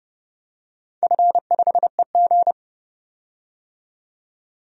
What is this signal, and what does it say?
Morse code, heard as a keyed tone, about 29 wpm, 700 Hz. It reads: F5EG
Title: DXCC-100 Call Signs - 4 Characters - 30wpm